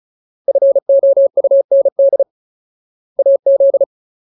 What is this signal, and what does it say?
Morse code, heard as a keyed tone, about 34 wpm, 550 Hz. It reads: FOUND AZ
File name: Ultimate Word List / 35WPM Ultimate